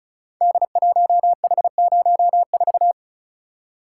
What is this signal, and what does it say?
Morse code, heard as a keyed tone, about 36 wpm, 700 Hz. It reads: D1H04
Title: Code Groups - 5 Letter:Numbers - 35wpm